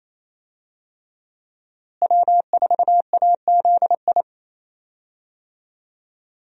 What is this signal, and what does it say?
Morse code, heard as a keyed tone, about 28 wpm, 700 Hz. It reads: W4AZI
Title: USA Call Signs - 5 Characters - Repeated 3x - 28wpm